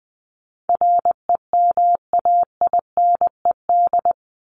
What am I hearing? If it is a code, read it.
Morse code, heard as a keyed tone, about 20 wpm, 700 Hz. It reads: REMAINED
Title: ED Suffix - Top 100 Words with Base Words 20wpm